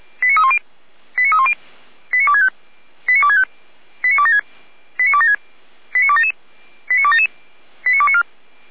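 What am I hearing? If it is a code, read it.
Five-tone selective call (ZVEI), heard as quick runs of five tones, tones of about 70 ms. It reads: 89310 89310 89265 89265 89276 89276 89277 89277 89294
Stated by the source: Five-tone ZVEI selective call received by SDR, from the audio server of Feuerwehr Orlamünde.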